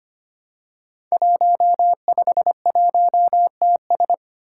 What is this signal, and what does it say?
Morse code, heard as a keyed tone, about 25 wpm, 700 Hz. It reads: 151TS
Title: Five Challenging Letter-Number Combinations 25wpm